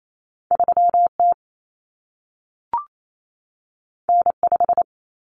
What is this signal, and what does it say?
Morse code, heard as a keyed tone, about 28 wpm, 700 Hz. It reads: 3T N5
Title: Common Cut Numbers - 2-Digit 28wpm